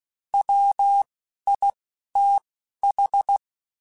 Morse code, heard as a keyed tone, about 16 wpm, spaced as for 11 wpm, 800 Hz. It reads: WITH
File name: Quotes Spaced / Quotes Spaced-mp3s / Airtight sp @15